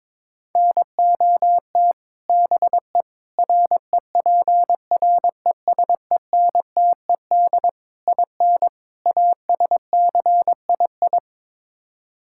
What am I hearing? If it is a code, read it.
Morse code, heard as a keyed tone, about 22 wpm, 700 Hz. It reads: NOT BE REPRESENTED IN ASCII
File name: practice / 22wpm / Build A Phrase - 5-Word Phrases - 22wpm